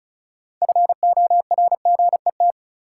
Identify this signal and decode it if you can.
Morse code, heard as a keyed tone, about 34 wpm, 700 Hz. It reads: FORGET
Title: Sentences from Top 1200 Words - New Word Drill Alt - 35wpm